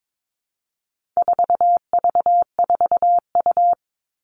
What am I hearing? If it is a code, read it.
Morse code, heard as a keyed tone, about 22 wpm, 700 Hz. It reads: 4V4U
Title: All Cut Numbers - 4-Digit 22wpm